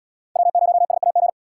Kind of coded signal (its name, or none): Morse code